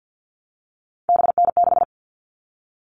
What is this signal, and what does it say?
Morse code, heard as a keyed tone, about 50 wpm, 700 Hz. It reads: BN6
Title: All Cut Numbers - 3-Digit 50wpm